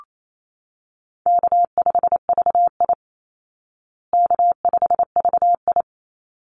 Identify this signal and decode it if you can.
Morse code, heard as a keyed tone, about 28 wpm, 700 Hz. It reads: K5VI K5VI
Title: Easy Call Signs Repeated 2x - 28wpm